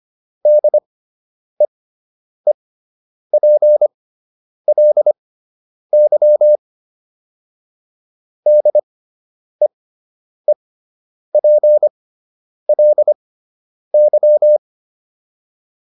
Morse code, heard as a keyed tone, about 25 wpm, spaced as for 9 wpm, 600 Hz. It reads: DEEPLY DEEPLY